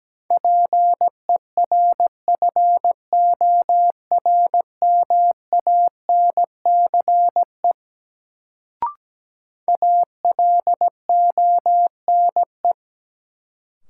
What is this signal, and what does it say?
Morse code, heard as a keyed tone, about 17 wpm, 700 Hz. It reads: PERFORMANCE ALONE
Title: Sentences from Top 1000 Words - New Word Drill - 17wpm